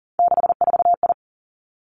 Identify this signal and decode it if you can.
Morse code, heard as a keyed tone, about 38 wpm, 700 Hz. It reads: 64I
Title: Three Challenging Letter-Number Combinations 40wpm